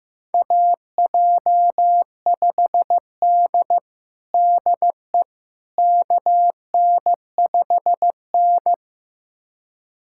Morse code, heard as a keyed tone, about 16 wpm, 700 Hz. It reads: AJ5D DE KN5N
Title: USA Callsign DE Callsign - 4 Characters - 15wpm